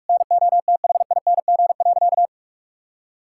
Morse code, heard as a keyed tone, about 46 wpm, 700 Hz. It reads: NOTHING.
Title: Sentences from Top 1000 Words - Review Part 6 - 45wpm